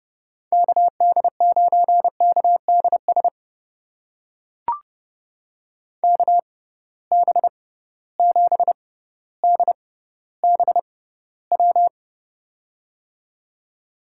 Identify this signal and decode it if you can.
Morse code, heard as a keyed tone, about 31 wpm, spaced as for 15 wpm, 700 Hz. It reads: KD9KDS KB7DBW
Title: USA Call Signs - 6 Characters - 6x Character Spacing - 30wpm